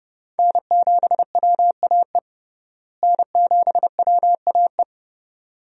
Morse code, heard as a keyed tone, about 30 wpm, 700 Hz. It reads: N7WAE N7WAE
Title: USA Call Signs - 5 Characters - Repeated 2x - 30wpm